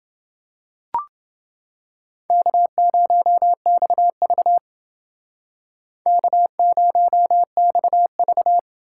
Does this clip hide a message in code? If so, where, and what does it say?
Yes: Morse code from 2.3 to 8.6 s, heard as a keyed tone, about 31 wpm, 700 Hz. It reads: K0XV K0XV